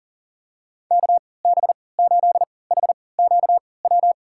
Morse code, heard as a keyed tone, about 38 wpm, spaced as for 22 wpm, 700 Hz. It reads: KB8HQW
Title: USA Call Signs - 6 Characters - 3x Character Spacing - 40wpm